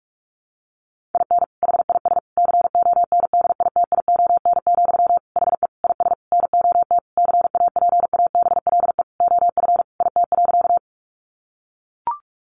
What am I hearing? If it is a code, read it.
Morse code, heard as a keyed tone, about 44 wpm, 700 Hz. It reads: IN HIS CONDITION, HE IS NOT CAPABLE OF IT.
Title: Sentences from Top 1200 Words 45wpm